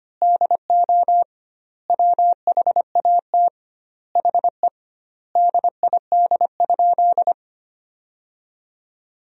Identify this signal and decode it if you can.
Morse code, heard as a keyed tone, about 25 wpm, 700 Hz. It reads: DO WHAT HE DID?
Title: 2x Word-Spacing - Sentences from Top 100 Words - 25wpm